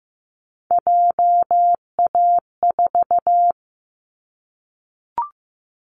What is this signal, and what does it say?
Morse code, heard as a keyed tone, about 15 wpm, 700 Hz. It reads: JA4
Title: Callsign Prefixes for Top 25 DXCC - Repeated 3x - 15wpm